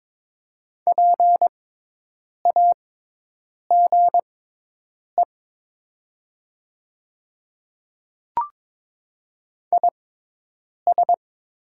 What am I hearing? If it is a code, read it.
Morse code, heard as a keyed tone, about 22 wpm, spaced as for 8 wpm, 700 Hz. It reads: PAGE IS
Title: 6x Character Spacing - Top 500 Words - 22wpm